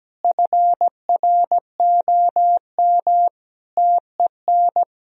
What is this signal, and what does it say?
Morse code, heard as a keyed tone, about 17 wpm, 700 Hz. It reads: FROM TEN
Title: Sentences from Top 1000 Words - Review Part 2 - 17wpm